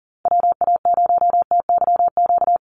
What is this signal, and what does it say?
Morse code, heard as a keyed tone, about 40 wpm, 700 Hz. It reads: WA0TYQ
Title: USA Call Signs Repeated 2x 40wpm